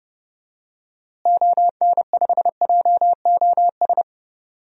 Morse code, heard as a keyed tone, about 31 wpm, 700 Hz. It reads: ON5JOS
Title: DXCC-100 Call Signs - 6 Characters - Repeated 2x - 30wpm